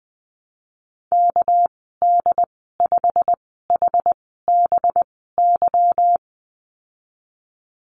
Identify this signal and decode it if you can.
Morse code, heard as a keyed tone, about 20 wpm, spaced as for 15 wpm, 700 Hz. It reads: KD5HBY
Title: 238 Lesson 38g - Letter q - Call Signs 20wpm